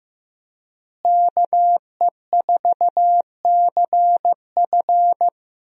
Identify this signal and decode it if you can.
Morse code, heard as a keyed tone, about 15 wpm, 700 Hz. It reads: KE4CF
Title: USA Call Signs - 5 Characters - Repeated 2x - 15wpm